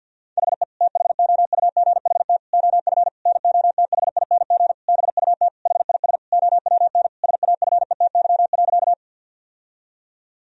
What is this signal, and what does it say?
Morse code, heard as a keyed tone, about 50 wpm, 700 Hz. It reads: HE THOUGHT OF NOTHING BUT HIS OWN SAFETY.
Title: Sentences from Top 1000 Words - Review Part 4 - 50wpm